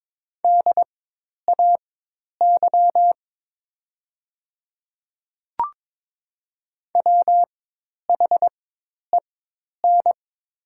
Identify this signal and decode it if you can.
Morse code, heard as a keyed tone, about 22 wpm, spaced as for 10 wpm, 700 Hz. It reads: DAY WHEN
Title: 4x Character Spacing - Top 100 Words - 22wpm